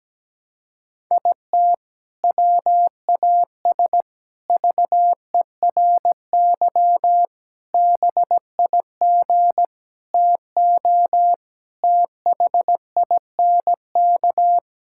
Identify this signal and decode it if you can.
Morse code, heard as a keyed tone, about 17 wpm, 700 Hz. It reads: IT WAS VERY BIG TO THINK
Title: Sentences from Top 1600 Words 17wpm